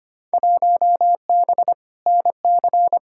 Morse code, heard as a keyed tone, about 25 wpm, 700 Hz. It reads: 1B NC
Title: Field Day Exchanges - 25wpm